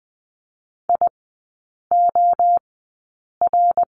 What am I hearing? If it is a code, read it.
Morse code, heard as a keyed tone, about 20 wpm, spaced as for 8 wpm, 700 Hz. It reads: IOR
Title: 053 Lesson 10e - Letter r - Three Characters 20wpm